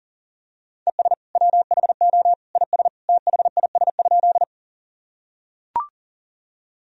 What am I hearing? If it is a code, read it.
Morse code, heard as a keyed tone, about 40 wpm, 700 Hz. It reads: ES WHO IS THIS?
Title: Sentences from Top 100 Words and Abbreviations 40wpm